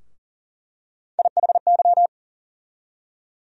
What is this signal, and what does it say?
Morse code, heard as a keyed tone, about 40 wpm, 700 Hz. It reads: IHY
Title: USA Call Sign Postfixes - 3 Letters 40wpm